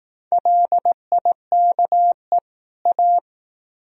Morse code, heard as a keyed tone, about 18 wpm, 700 Hz. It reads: LIKE A
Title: Speed-Racing - Sets of 2 Words 15wpm